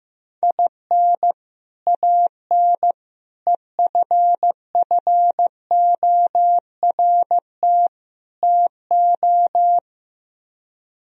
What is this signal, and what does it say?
Morse code, heard as a keyed tone, about 16 wpm, 700 Hz. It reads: IN AN EFFORT TO
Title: Speed-Racing - Sets of 4 Words 15wpm